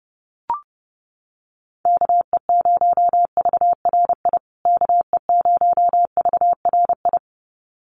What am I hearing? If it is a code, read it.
Morse code, heard as a keyed tone, about 30 wpm, 700 Hz. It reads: KE0VRI KE0VRI
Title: USA Call Signs Repeated 2x 30wpm